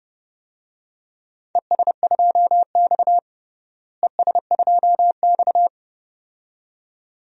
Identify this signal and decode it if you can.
Morse code, heard as a keyed tone, about 29 wpm, 700 Hz. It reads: ES2X ES2X
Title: DXCC-100 Call Signs - 4 Characters - Repeated 2x - 30wpm